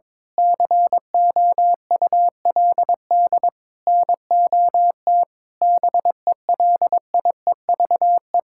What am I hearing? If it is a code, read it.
Morse code, heard as a keyed tone, about 22 wpm, 700 Hz. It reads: COULD NOT BELIEVE
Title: Sentences from Top 1000 Words - Review Part 5 - 22wpm